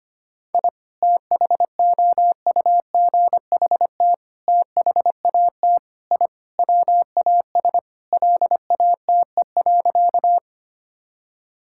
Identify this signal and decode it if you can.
Morse code, heard as a keyed tone, about 25 wpm, 700 Hz. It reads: I THOUGHT THAT I WAS LATE.
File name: practice / 25wpm / Sentences with 3-Word Phrases - 25wpm